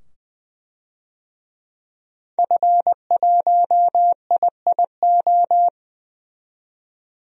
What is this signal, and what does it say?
Morse code, heard as a keyed tone, about 20 wpm, 700 Hz. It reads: F1IIO
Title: DXCC-100 Call Signs - 5 Characters - 20wpm